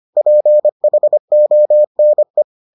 Morse code, heard as a keyed tone, about 25 wpm, 600 Hz. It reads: PHONE